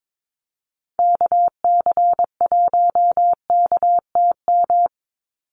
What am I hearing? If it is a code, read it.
Morse code, heard as a keyed tone, about 22 wpm, 700 Hz. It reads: KC1KTM
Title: USA Call Signs 22wpm